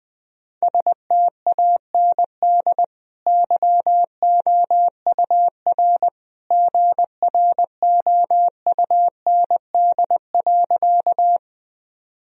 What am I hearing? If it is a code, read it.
Morse code, heard as a keyed tone, about 21 wpm, 700 Hz. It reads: STAND YOUR GROUND.
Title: Sentences from Top 1000 Words - Review Part 2 - 20wpm